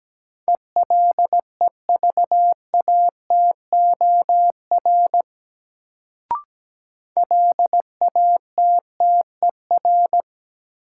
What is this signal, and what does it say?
Morse code, heard as a keyed tone, about 17 wpm, 700 Hz. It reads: ELEVATOR LATTER
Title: Sentences from Top 1600 Words - New Word Drill - 17wpm